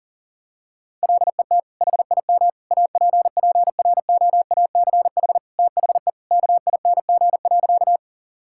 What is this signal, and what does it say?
Morse code, heard as a keyed tone, about 40 wpm, 700 Hz. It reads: LET HIM APPROACH THE KING.
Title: Sentences from Top 1000 Words - Review Part 6 - 40wpm